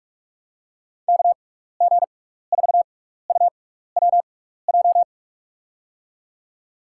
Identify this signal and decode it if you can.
Morse code, heard as a keyed tone, about 46 wpm, spaced as for 16 wpm, 700 Hz. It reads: KG4UWJ
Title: USA Call Signs - 6 Characters - 6x Character Spacing - 45wpm